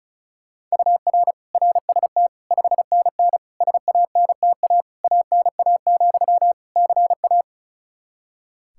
Morse code, heard as a keyed tone, about 36 wpm, 700 Hz. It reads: UR RST 5NN SANTA ANA, CA
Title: Practice QSO - Exchange 1 - Beginning - 35wpm